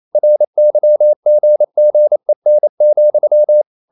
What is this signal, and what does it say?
Morse code, heard as a keyed tone, about 28 wpm, 600 Hz. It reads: RYGGEN,
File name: Ferie QRQ 28wpm